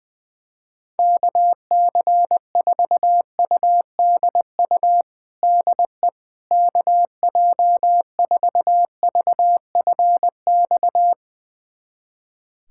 Morse code, heard as a keyed tone, about 20 wpm, 700 Hz. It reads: KC4UDU DE KJ4VFX
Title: USA Callsign DE Callsign - 6 Characters - 20wpm